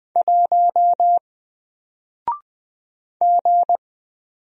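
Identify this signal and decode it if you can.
Morse code, heard as a keyed tone, about 20 wpm, 700 Hz. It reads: 1 G
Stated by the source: Single letter and number Morse drill at 20 wpm, from N4LCD.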